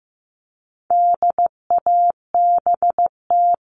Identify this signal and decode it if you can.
Morse code, heard as a keyed tone, about 15 wpm, 700 Hz. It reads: DABT
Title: All Cut Numbers - 4-Digit 15wpm